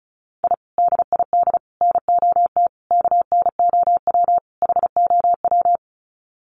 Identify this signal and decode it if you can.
Morse code, heard as a keyed tone, about 34 wpm, 700 Hz. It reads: I DID NOT KNOW HOW
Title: Sets of 5 Words Encore 35wpm